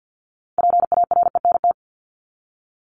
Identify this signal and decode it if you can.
Morse code, heard as a keyed tone, about 50 wpm, 700 Hz. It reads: PARENT
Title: Top 1751-2000 Words 50wpm